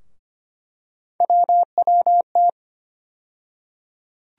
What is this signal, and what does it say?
Morse code, heard as a keyed tone, about 25 wpm, 700 Hz. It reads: WWT
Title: USA Call Sign Postfixes - 3 Letters 25wpm